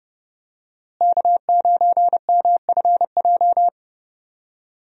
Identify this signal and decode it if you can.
Morse code, heard as a keyed tone, about 30 wpm, 700 Hz. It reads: K9MFJ
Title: USA Call Signs - 5 Characters - Repeated 3x - 30wpm